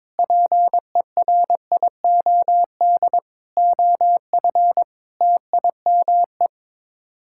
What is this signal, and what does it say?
Morse code, heard as a keyed tone, about 22 wpm, 700 Hz. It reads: PERIOD OF TIME
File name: practice / 22wpm / Sets of 6 Words - 22wpm